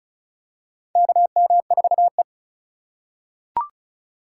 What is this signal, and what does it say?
Morse code, heard as a keyed tone, about 36 wpm, 700 Hz. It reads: KM4E
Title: USA Call Signs - 4 Characters - 3x Character Spacing - 35wpm